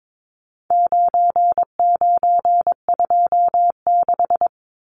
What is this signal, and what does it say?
Morse code, heard as a keyed tone, about 22 wpm, 700 Hz. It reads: 9926
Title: Numbers - 4-Digit 22wpm